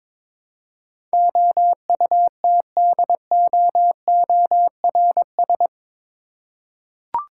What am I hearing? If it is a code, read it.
Morse code, heard as a keyed tone, about 22 wpm, 700 Hz. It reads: OUTDOORS
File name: practice / 22wpm / Eight Letter Words - 22wpm